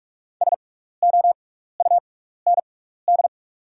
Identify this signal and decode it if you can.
Morse code, heard as a keyed tone, about 45 wpm, spaced as for 15 wpm, 690 Hz. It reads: SOUND
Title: 6x Character Spacing - Top 100 Words - 45wpm